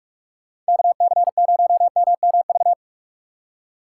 Morse code, heard as a keyed tone, about 59 wpm, spaced as for 52 wpm, 700 Hz. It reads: KC0MMV
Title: Speed Racing - USA 2x3 Call Signs 45wpm